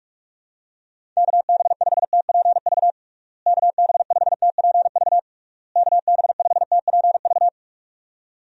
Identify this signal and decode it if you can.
Morse code, heard as a keyed tone, about 46 wpm, 700 Hz. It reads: KB5TPV KB5TPV KB5TPV
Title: USA Call Signs - 6 Characters - Repeated 3x - 45wpm